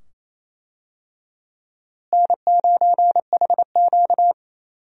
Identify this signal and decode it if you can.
Morse code, heard as a keyed tone, about 28 wpm, 700 Hz. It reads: N9HQ
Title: USA Call Signs - 4 Characters - Repeated 2x - 28wpm